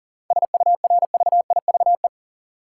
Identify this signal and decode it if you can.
Morse code, heard as a keyed tone, about 39 wpm, 700 Hz. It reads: SURVIVE
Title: Sentences from Top 1300 Words - New Word Drill - 40wpm